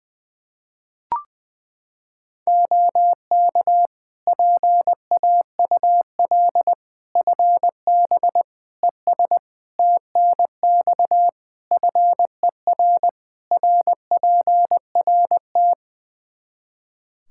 Morse code, heard as a keyed tone, about 20 wpm, 700 Hz. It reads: OK PAUL FB ES TNX FER RPRT
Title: K1USN SST QSO Practice 20wpm